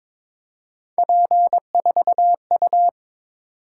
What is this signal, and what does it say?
Morse code, heard as a keyed tone, about 22 wpm, 700 Hz. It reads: P4U
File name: practice / 22wpm / Three Letter-Number Combinations - 22wpm